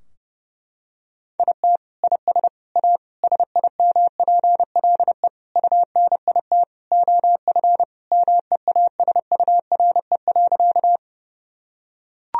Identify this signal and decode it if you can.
Morse code, heard as a keyed tone, about 30 wpm, 700 Hz. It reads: IT IS A SIMPLE UNIT OF MEASURE.
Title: Sentences from Top 1000 Words - Review Part 2 - 30wpm